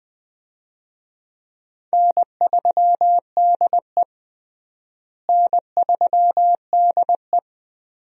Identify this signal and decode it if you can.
Morse code, heard as a keyed tone, about 20 wpm, 700 Hz. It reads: N3DE N3DE
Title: USA Call Signs - 4 Characters - Repeated 3x - 20wpm